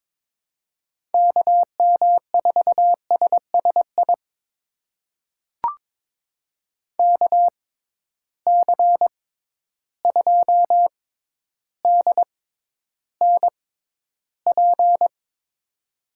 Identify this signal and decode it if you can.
Morse code, heard as a keyed tone, about 22 wpm, spaced as for 11 wpm, 700 Hz. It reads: KM4SSI KC2DNP
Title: USA Call Signs - 6 Characters - 6x Character Spacing - 22wpm